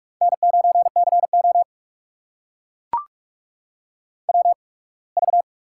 Morse code, heard as a keyed tone, about 44 wpm, spaced as for 27 wpm, 690 Hz. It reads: N9CO WV
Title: USA Call Signs - 4 Characters - 8x Character Spacing - 45wpm